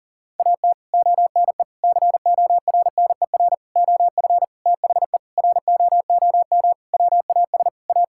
Morse code, heard as a keyed tone, about 40 wpm, 700 Hz. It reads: AT ONE CORNER OF THE ROOM WAS A
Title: Sentences from Top 1600 Words 40wpm